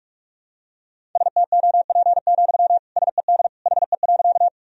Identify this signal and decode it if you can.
Morse code, heard as a keyed tone, about 46 wpm, 700 Hz. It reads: STOP, SED HE.
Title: Sentences from Top 300 Words and Abbreviations 45wpm